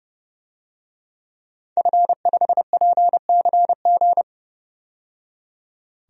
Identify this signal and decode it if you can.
Morse code, heard as a keyed tone, about 32 wpm, 700 Hz. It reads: F5PCG
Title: DXCC-100 Call Signs - 5 Characters - 30wpm